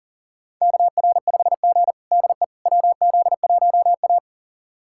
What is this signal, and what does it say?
Morse code, heard as a keyed tone, about 37 wpm, 700 Hz. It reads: KR5G DE WZ1A
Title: USA Callsign DE Callsign - 4 Characters - 40wpm